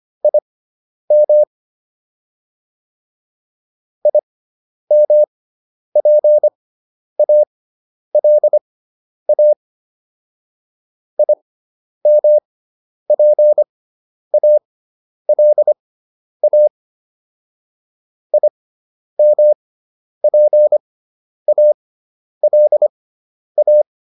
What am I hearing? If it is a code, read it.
Morse code, heard as a keyed tone, about 25 wpm, spaced as for 10 wpm, 600 Hz. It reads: IM IMPALA IMPALA IMPALA
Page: intermediate-practice-files